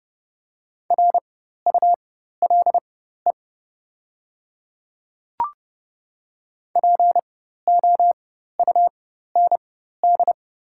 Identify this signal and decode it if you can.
Morse code, heard as a keyed tone, about 30 wpm, spaced as for 14 wpm, 700 Hz. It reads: RULE POUND
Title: 4x Character Spacing - Top 500 Words - 30wpm